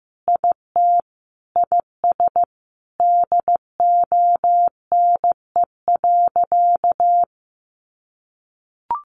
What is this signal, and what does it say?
Morse code, heard as a keyed tone, about 14 wpm, 700 Hz. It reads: IT IS DONE.
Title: Sentences from Top 1500 Words 15wpm